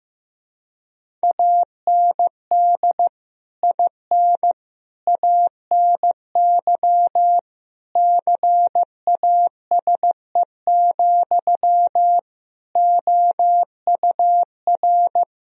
Morse code, heard as a keyed tone, about 16 wpm, 700 Hz. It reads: AND IN ANY CASE, OUR
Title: Sentences from Top 1000 Words - Review Part 1 - 15wpm